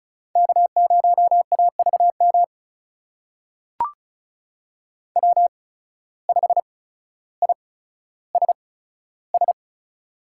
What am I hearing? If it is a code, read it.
Morse code, heard as a keyed tone, about 36 wpm, spaced as for 15 wpm, 700 Hz. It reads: K0AVM W5ISS
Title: USA Call Signs - 5 Characters - 8x Character Spacing - 35wpm